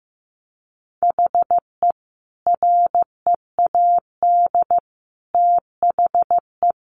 Morse code, heard as a keyed tone, about 15 wpm, 700 Hz. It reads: HE READ THE
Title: Sentences from Top 600 Words 15wpm